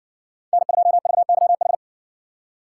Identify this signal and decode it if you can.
Morse code, heard as a keyed tone, about 55 wpm, 700 Hz. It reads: N2VYH